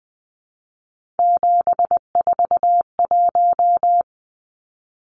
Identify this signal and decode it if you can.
Morse code, heard as a keyed tone, about 20 wpm, 700 Hz. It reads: 741
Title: Numbers - 3-Digit 20wpm